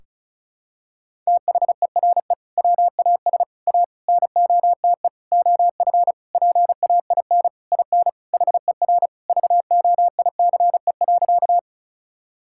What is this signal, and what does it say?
Morse code, heard as a keyed tone, about 36 wpm, 690 Hz. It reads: THERE WAS A NOTE OF PAIN IN HER VOICE.
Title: Sentences from Top 1600 Words 35wpm